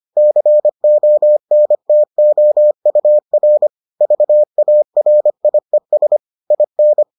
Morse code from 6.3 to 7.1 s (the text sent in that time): IN